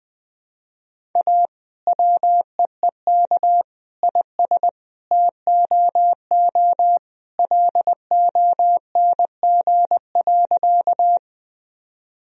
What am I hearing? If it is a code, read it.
Morse code, heard as a keyed tone, about 21 wpm, 700 Hz. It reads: A WEEK IS TOO LONG.
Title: Sentences from Top 1000 Words - Review Part 2 - 20wpm